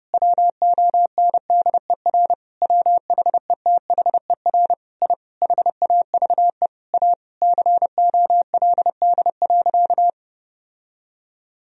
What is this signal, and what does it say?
Morse code, heard as a keyed tone, about 32 wpm, 700 Hz. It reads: WONDER WHETHER I HAVE A COLD.
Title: Sentences from Top 1000 Words - Review Part 3 - 30wpm